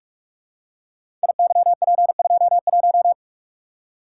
Morse code, heard as a keyed tone, about 47 wpm, 700 Hz. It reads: IYP21
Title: Code Groups - 5 Letter:Numbers - 45wpm